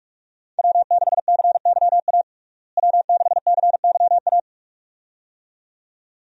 Morse code, heard as a keyed tone, about 46 wpm, 700 Hz. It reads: W6CYA W6CYA
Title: USA Call Signs - 5 Characters - Repeated 2x - 45wpm